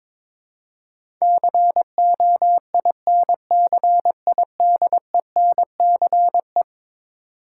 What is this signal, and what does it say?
Morse code, heard as a keyed tone, about 23 wpm, 700 Hz. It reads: COINCIDENCE
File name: practice / 22wpm / Eleven Letter Words - 22wpm